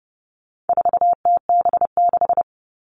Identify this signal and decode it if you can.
Morse code, heard as a keyed tone, about 30 wpm, 700 Hz. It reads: 4TB6
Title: All Cut Numbers - 4-Digit 30wpm